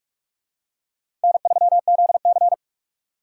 Morse code, heard as a keyed tone, about 44 wpm, 700 Hz. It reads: N3ZC
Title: USA Call Signs - 4 Characters - 6x Character Spacing - 45wpm